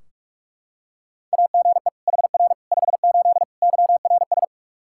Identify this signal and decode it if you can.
Morse code, heard as a keyed tone, about 46 wpm, 700 Hz. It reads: AGE HR 58 YRS
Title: Practice QSO - Exchange 3 - More Info - 45wpm